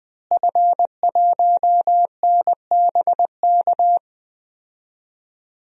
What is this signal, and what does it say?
Morse code, heard as a keyed tone, about 20 wpm, 700 Hz. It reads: F1NBK
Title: DXCC-100 Call Signs - 5 Characters - Repeated 2x - 20wpm